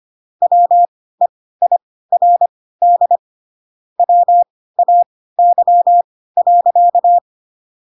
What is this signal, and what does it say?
Morse code, heard as a keyed tone, about 25 wpm, spaced as for 16 wpm, 700 Hz. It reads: WEIRD WAY.
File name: Pinocchio / Pinocchio ch6 10 25 ik1hge